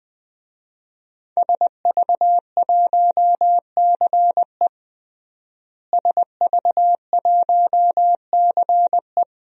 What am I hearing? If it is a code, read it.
Morse code, heard as a keyed tone, about 20 wpm, 700 Hz. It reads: SV1CE SV1CE
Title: DXCC-100 Call Signs - 5 Characters - Repeated 2x - 20wpm